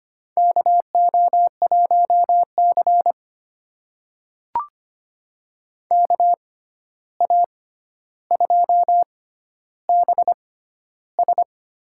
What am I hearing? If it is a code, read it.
Morse code, heard as a keyed tone, about 25 wpm, spaced as for 12 wpm, 700 Hz. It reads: KO1C KA2BS
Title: USA Call Signs - 6x Character Spacing - 25wpm